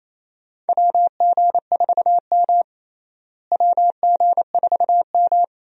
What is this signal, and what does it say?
Morse code, heard as a keyed tone, about 28 wpm, 700 Hz. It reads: WG4M WG4M
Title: USA Call Signs - 4 Characters - Repeated 2x - 28wpm